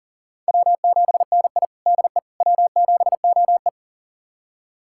Morse code, heard as a keyed tone, about 40 wpm, 700 Hz. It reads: W7NI DE W7OE
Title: USA Callsign DE Callsign - 4 Characters - 40wpm